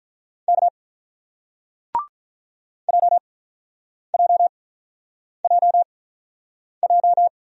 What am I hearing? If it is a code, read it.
Morse code, heard as a keyed tone, about 48 wpm, spaced as for 9 wpm, 700 Hz. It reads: K JJJJ